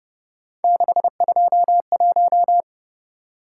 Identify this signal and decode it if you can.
Morse code, heard as a keyed tone, about 29 wpm, 700 Hz. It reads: 621